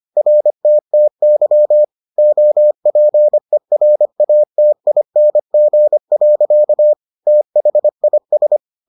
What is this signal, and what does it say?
Morse code, heard as a keyed tone, about 25 wpm, 600 Hz. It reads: RTTY OPERATING. THIS